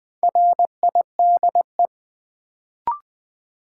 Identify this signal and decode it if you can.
Morse code, heard as a keyed tone, about 20 wpm, 700 Hz. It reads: RIDE
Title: Sentences from Top 1000 Words - Review Part 6 - New Word Drill - 20wpm